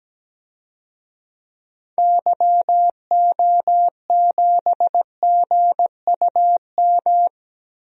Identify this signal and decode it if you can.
Morse code, heard as a keyed tone, about 17 wpm, 700 Hz. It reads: YO7GUM
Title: DXCC-100 Call Signs - 6 Characters - Repeated 2x - 17wpm